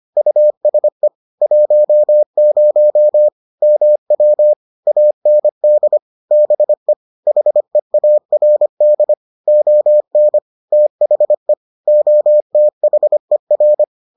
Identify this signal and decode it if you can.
Morse code, heard as a keyed tone, about 25 wpm, 600 Hz. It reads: USE 10 MW AND BE HEARD ON THE OTHER